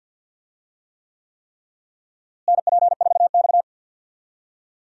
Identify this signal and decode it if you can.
Morse code, heard as a keyed tone, about 52 wpm, 700 Hz. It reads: NP4X
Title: USA Call Signs - 4 Characters - Repeated 3x - 50wpm